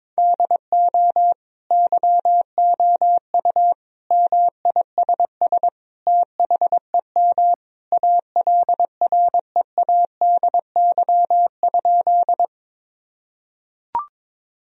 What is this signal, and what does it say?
Morse code, heard as a keyed tone, about 22 wpm, 700 Hz. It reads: DO YOU MISS THEM ALREADY?
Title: Sentences from Top 1000 Words - Review Part 3 - 22wpm